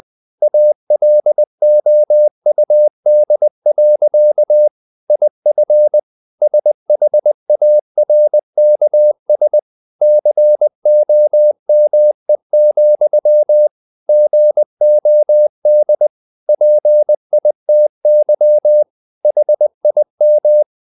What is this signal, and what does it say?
Morse code, heard as a keyed tone, about 20 wpm, 600 Hz. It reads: ALOUD. IF SHARKS COME, GOD PITY HIM
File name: the old man and the sea / the old man and the sea part 1 20wpm